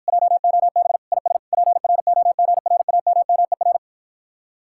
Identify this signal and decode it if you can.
Morse code, heard as a keyed tone, about 52 wpm, 700 Hz. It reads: JOB IS PROGRAMMER